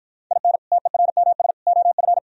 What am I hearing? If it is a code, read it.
Morse code, heard as a keyed tone, about 52 wpm, 700 Hz. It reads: IN TERMS OF